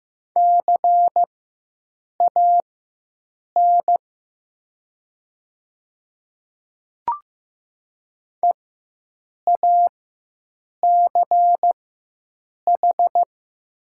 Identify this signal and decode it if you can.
Morse code, heard as a keyed tone, about 15 wpm, spaced as for 7 wpm, 700 Hz. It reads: CAN EACH
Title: 4x Character Spacing - Top 500 Words - 15wpm